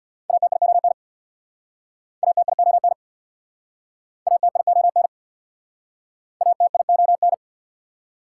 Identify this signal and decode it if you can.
Morse code, heard as a keyed tone, about 68 wpm, 690 Hz. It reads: ATION ATION ATION ATION